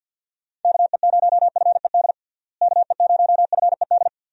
Morse code, heard as a keyed tone, about 52 wpm, 700 Hz. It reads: KE0FED KE0FED
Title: USA Call Signs - 6 Characters - Repeated 3x - 50wpm